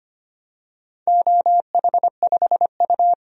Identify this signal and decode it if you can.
Morse code, heard as a keyed tone, about 25 wpm, 700 Hz. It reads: OH5U
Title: DXCC-100 Call Signs - 4 Characters - 25wpm